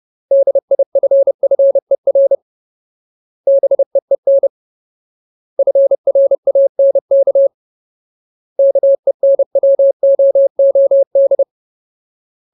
Morse code, heard as a keyed tone, about 30 wpm, 550 Hz. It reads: DIFFER BEEN FRANK KENWOOD
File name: Ultimate Word List / 30WPM Ultimate